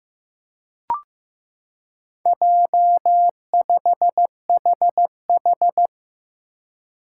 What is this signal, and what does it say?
Morse code, heard as a keyed tone, about 15 wpm, 700 Hz. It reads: J5HH
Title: Four Challenging Letter-Number Combinations 15wpm